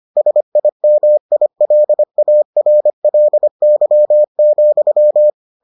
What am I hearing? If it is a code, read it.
Morse code, heard as a keyed tone, about 25 wpm, 600 Hz. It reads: SIMILARLY,